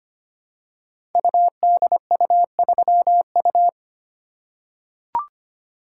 Morse code, heard as a keyed tone, about 25 wpm, 700 Hz. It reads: UDU3U
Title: Code Groups - 5 Letter:Numbers - 25wpm